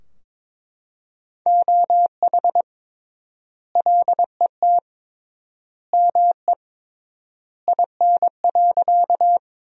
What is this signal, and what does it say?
Morse code, heard as a keyed tone, about 22 wpm, 700 Hz. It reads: OH LET ME IN.
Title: 3x Word-Spacing - Sentences from Top 500 Words - 22wpm